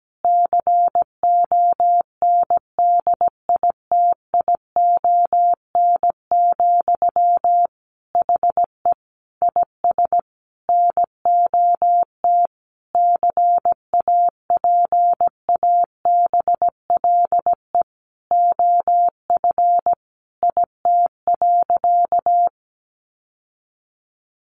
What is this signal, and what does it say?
Morse code, heard as a keyed tone, about 17 wpm, 700 Hz. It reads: CONDITION, HE IS NOT CAPABLE OF IT.
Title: Sentences from Top 1200 Words 17wpm